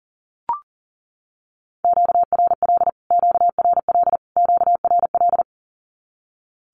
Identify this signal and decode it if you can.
Morse code, heard as a keyed tone, about 40 wpm, 700 Hz. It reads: QRL QRL QRL
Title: Q-Codes Repeated 3x 40wpm